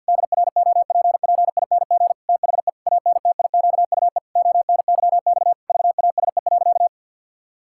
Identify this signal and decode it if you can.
Morse code, heard as a keyed tone, about 50 wpm, 700 Hz. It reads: DROPPING THE ANTIQUE ONYX VASE.
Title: Pangram Challenge v2 - 50wpm